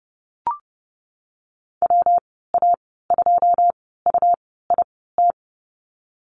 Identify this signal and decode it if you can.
Morse code, heard as a keyed tone, about 32 wpm, 700 Hz. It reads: W A 2 U I T
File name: Lesson 19g - Letter w - Call Signs 30wpm